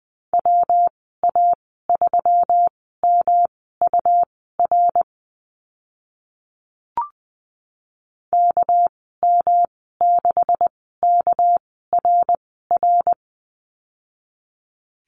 Morse code, heard as a keyed tone, about 20 wpm, spaced as for 15 wpm, 700 Hz. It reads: WA3MUR KM6KRR
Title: 207 Lesson 33g - Letter j - Call Signs 20wpm